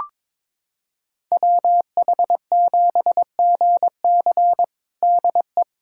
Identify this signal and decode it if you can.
Morse code, heard as a keyed tone, about 22 wpm, 700 Hz. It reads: WH7GC DE
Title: USA Callsign DE Callsign - 5 Characters - 22wpm